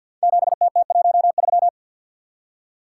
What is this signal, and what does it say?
Morse code, heard as a keyed tone, about 52 wpm, 700 Hz. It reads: 7TT13
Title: Code Groups - 5 Letter:Numbers - 50wpm